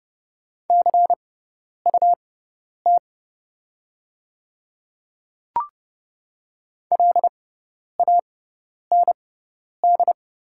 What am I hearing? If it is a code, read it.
Morse code, heard as a keyed tone, about 30 wpm, spaced as for 10 wpm, 700 Hz. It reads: CUT LAND
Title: 6x Character Spacing - Top 300 Words - 30wpm